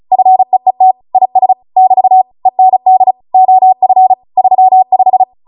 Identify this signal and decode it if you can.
Morse code, heard as a keyed tone, about 36 wpm, 750 Hz. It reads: FEET IS <BT> END OF 35